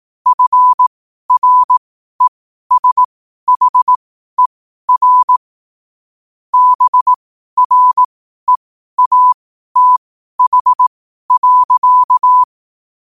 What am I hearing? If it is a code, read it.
Morse code, heard as a keyed tone, about 18 wpm, spaced as for 12 wpm, 1000 Hz. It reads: FRESHER BREATH.
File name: qotdfiles / 12wpm